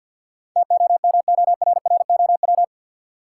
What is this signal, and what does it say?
Morse code, heard as a keyed tone, about 53 wpm, 700 Hz. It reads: TOMORROW
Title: Sentences from Top 1100 Words - New Word Drill Alt - 50wpm